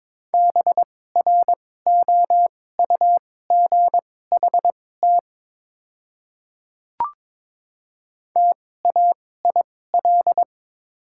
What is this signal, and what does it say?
Morse code, heard as a keyed tone, about 21 wpm, spaced as for 16 wpm, 700 Hz. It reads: BROUGHT TAIL
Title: 2x Character Spacing - Top 500 Words - 22wpm